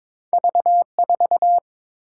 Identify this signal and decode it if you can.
Morse code, heard as a keyed tone, about 30 wpm, spaced as for 26 wpm, 700 Hz. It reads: V4
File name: Speed-Racing - Two Letter-Number Combinations - 22wpm